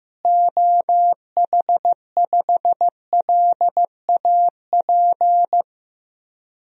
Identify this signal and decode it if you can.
Morse code, heard as a keyed tone, about 15 wpm, 700 Hz. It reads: OH5LAP
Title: DXCC-100 Call Signs - 6 Characters - Repeated 3x - 15wpm